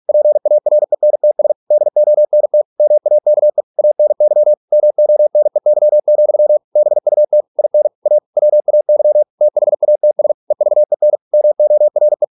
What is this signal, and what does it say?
Morse code, heard as a keyed tone, about 46 wpm, 600 Hz. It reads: PARENTS DONT MAKE ANY MONEY, BUT IN A WAY THATS EVEN MORE